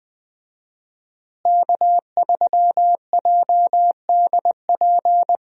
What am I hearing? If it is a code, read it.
Morse code, heard as a keyed tone, about 20 wpm, 700 Hz. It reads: K3JDP
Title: USA Call Signs - 5 Characters - Repeated 2x - 20wpm